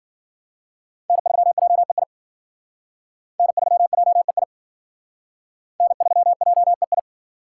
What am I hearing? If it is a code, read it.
Morse code, heard as a keyed tone, about 53 wpm, 700 Hz. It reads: N3JEI N3JEI N3JEI